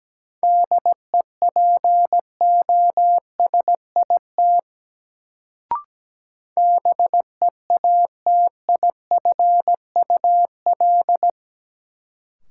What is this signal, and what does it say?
Morse code, heard as a keyed tone, about 17 wpm, 700 Hz. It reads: DEPOSIT BEATIFUL
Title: Sentences from Top 1300 Words - New Word Drill Alt - 17wpm